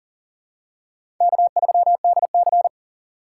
Morse code, heard as a keyed tone, about 40 wpm, 700 Hz. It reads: K3DC
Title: USA Call Signs - 4 Characters - 2x Character Spacing - 40wpm